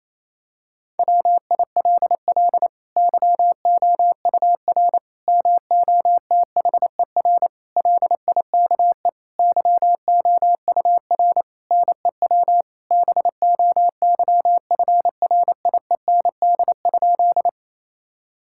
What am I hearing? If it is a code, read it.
Morse code, heard as a keyed tone, about 27 wpm, 700 Hz. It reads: WILL YOUR MOTHER LIKE YOUR NEW BOYFRIEND?
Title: Sentences from Top 1000 Words - Review Part 5 - 28wpm